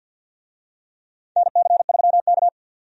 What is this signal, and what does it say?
Morse code, heard as a keyed tone, about 48 wpm, 700 Hz. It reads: NC3K
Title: USA Call Signs - 4 Characters - 50wpm